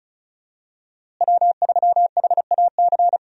Morse code, heard as a keyed tone, about 36 wpm, 700 Hz. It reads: W3HAC
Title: USA Call Signs - 5 Characters - Repeated 3x - 35wpm